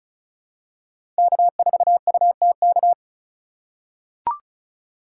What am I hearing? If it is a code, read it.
Morse code, heard as a keyed tone, about 36 wpm, 700 Hz. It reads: K4UTK
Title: USA Call Signs - 5 Characters - 2x Character Spacing - 35wpm